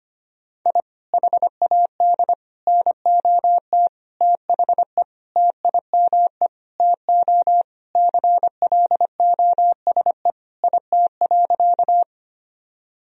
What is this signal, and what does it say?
Morse code, heard as a keyed tone, about 25 wpm, 700 Hz. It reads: I HAD NOT THE TIME TO CLOSE IT.
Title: Sentences from Top 1000 Words - Review Part 1 - 25wpm